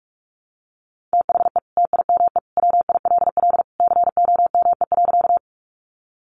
Morse code, heard as a keyed tone, about 44 wpm, 700 Hz. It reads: THE TIME WILL COME.
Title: Sentences from Top 100 Words 45wpm